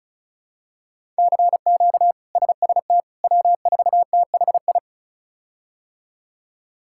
Morse code, heard as a keyed tone, about 36 wpm, 700 Hz. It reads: CQ SST W4THI
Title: K1USN SST QSO Practice - 35wpm